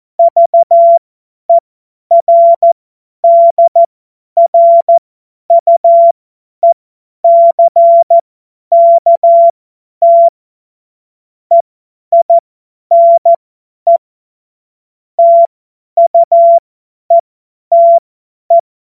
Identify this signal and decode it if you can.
Morse code, heard as a keyed tone, about 14 wpm, spaced as for 10 wpm, 680 Hz. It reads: VERDRUECKT EINE TUETE
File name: Morsekurs / Witze / Witz 13. Januar 2026 50 F70